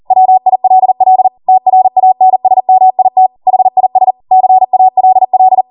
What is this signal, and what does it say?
Morse code, heard as a keyed tone, about 40 wpm, 750 Hz. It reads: WILL TRANSMIT HIS CALL